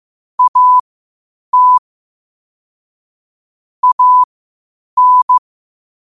Morse code, heard as a keyed tone, about 15 wpm, spaced as for 8 wpm, 1000 Hz. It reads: AT AN
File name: qotdfiles / 8wpm